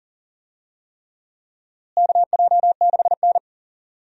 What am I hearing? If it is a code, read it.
Morse code, heard as a keyed tone, about 40 wpm, 700 Hz. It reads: KJ6N
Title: USA Call Signs - 4 Characters - Repeated 3x - 40wpm